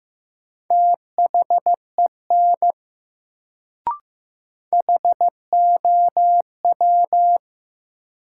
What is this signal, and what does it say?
Morse code, heard as a keyed tone, about 15 wpm, 700 Hz. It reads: THEN HOW
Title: Sentences from Top 100 Words - New Word Drill - 15wpm